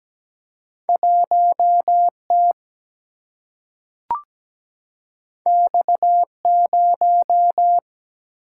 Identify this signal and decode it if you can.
Morse code, heard as a keyed tone, about 17 wpm, 700 Hz. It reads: 1T X0
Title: Single-Letter Single-Number 17wpm